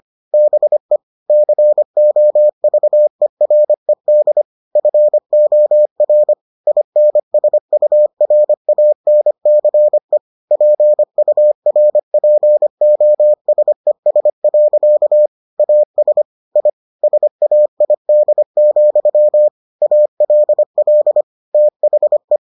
Morse code, heard as a keyed tone, about 25 wpm, 600 Hz. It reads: BE COVERED FOR INSURANCE PURPOSES. AS I SAID, ALL THE